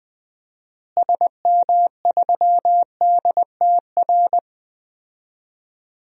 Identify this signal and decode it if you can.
Morse code, heard as a keyed tone, about 21 wpm, 700 Hz. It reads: SM3DTR
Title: DXCC-100 Call Signs - 6 Characters - Repeated 3x - 20wpm